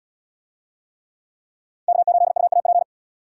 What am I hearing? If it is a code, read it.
Morse code, heard as a keyed tone, about 66 wpm, 700 Hz. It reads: K8UAP